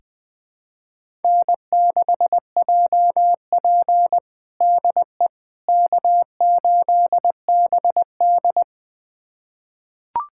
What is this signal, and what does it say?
Morse code, heard as a keyed tone, about 21 wpm, 700 Hz. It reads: N6JP DE K8BD
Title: USA Callsign DE Callsign - 4 Characters - 20wpm